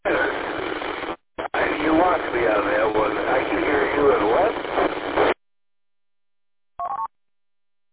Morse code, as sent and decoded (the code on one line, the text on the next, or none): none